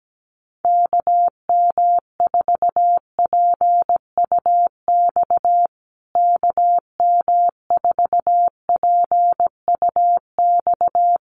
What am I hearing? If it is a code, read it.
Morse code, heard as a keyed tone, about 17 wpm, 700 Hz. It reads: KM4PUX KM4PUX
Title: USA Call Signs Repeated 3x 17wpm